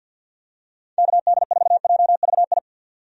Morse code, heard as a keyed tone, about 50 wpm, 700 Hz. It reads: KD4JVI
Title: USA Call Signs - 6 Characters - 50wpm